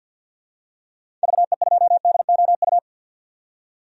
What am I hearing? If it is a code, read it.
Morse code, heard as a keyed tone, about 48 wpm, 700 Hz. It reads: VE2DOU
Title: DXCC-100 Call Signs - 6 Characters - 50wpm